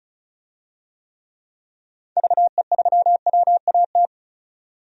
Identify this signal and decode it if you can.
Morse code, heard as a keyed tone, about 34 wpm, 700 Hz. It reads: VE3WAT
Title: DXCC-100 Call Signs - 6 Characters - 35wpm